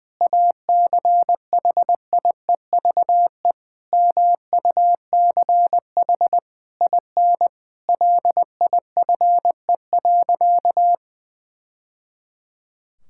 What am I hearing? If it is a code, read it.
Morse code, heard as a keyed tone, about 21 wpm, 700 Hz. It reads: ACHIEVE MUCH IN LIFE.
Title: Sentences from Top 1000 Words - Review Part 6 - 20wpm